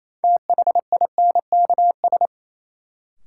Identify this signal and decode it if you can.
Morse code, heard as a keyed tone, about 28 wpm, 700 Hz. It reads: THINKS
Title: Sentences from Top 1000 Words - Review Part 6 - New Word Drill - 28wpm